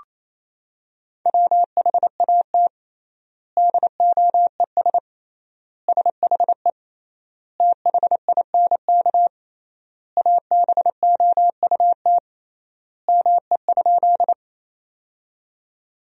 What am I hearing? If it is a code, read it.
Morse code, heard as a keyed tone, about 28 wpm, 700 Hz. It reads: WHAT DOES SHE THINK ABOUT ME?
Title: 3x Word-Spacing - Sentences from Top 200 Words - 28wpm